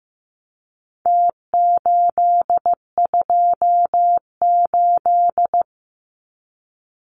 Morse code, heard as a keyed tone, about 15 wpm, 700 Hz. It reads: T828
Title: Numbers - Contest Style - 4-Digit 15wpm